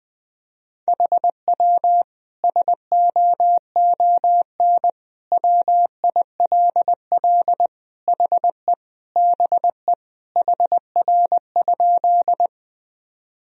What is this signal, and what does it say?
Morse code, heard as a keyed tone, about 20 wpm, 700 Hz. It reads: HW SOON WILL HE BE HR?
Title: Sentences from Top 400 Words and Abbreviations 20wpm